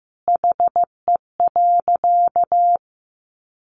Morse code, heard as a keyed tone, about 15 wpm, 700 Hz. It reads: HE.
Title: Sentences from Top 1500 Words 15wpm